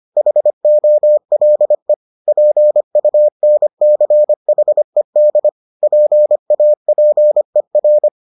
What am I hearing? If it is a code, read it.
Morse code, heard as a keyed tone, about 25 wpm, 600 Hz. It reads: HOLE PUNCHED PAPER